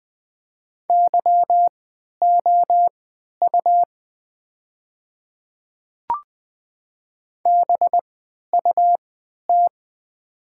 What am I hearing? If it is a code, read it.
Morse code, heard as a keyed tone, about 20 wpm, spaced as for 11 wpm, 700 Hz. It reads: YOU BUT
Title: 3x Character Spacing - Top 100 Words - 20wpm